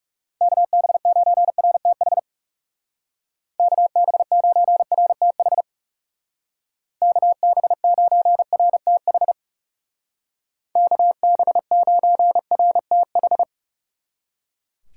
Morse code, heard as a keyed tone, about 35 wpm, 700 Hz. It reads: KB9RTH KB9RTH KB9RTH KB9RTH
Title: Speed Racing - USA 2x3 Call Signs 30wpm